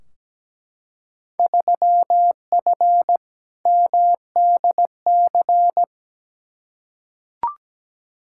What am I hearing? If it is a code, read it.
Morse code, heard as a keyed tone, about 17 wpm, 700 Hz. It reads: 3F MDC
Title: Field Day Exchanges Encore - 17wpm